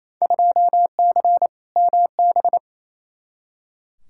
Morse code, heard as a keyed tone, about 28 wpm, 700 Hz. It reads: 2C MB
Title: Field Day Exchanges Encore - 28wpm